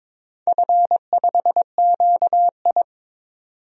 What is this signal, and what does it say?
Morse code, heard as a keyed tone, about 23 wpm, 700 Hz. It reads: F5QI
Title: DXCC-100 Call Signs - 4 Characters - 22wpm